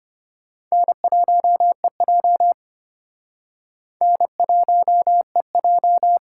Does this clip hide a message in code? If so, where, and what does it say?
Yes: Morse code from 0.7 to 6.2 s, heard as a keyed tone, about 32 wpm, 700 Hz. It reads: N1EJ N1EJ